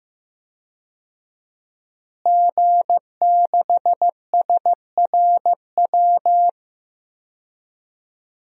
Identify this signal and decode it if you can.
Morse code, heard as a keyed tone, about 15 wpm, 700 Hz. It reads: G6SRW
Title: DXCC-100 Call Signs - 5 Characters - Repeated 2x - 15wpm